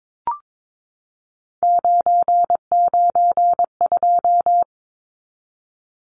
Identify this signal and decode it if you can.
Morse code, heard as a keyed tone, about 22 wpm, 700 Hz. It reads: 992
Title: Numbers - 3-Digit 22wpm